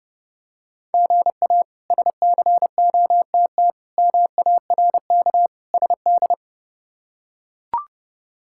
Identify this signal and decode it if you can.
Morse code, heard as a keyed tone, about 30 wpm, 700 Hz. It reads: GA SCOTT MARK SD
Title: K1USN SST QSO Practice - 30wpm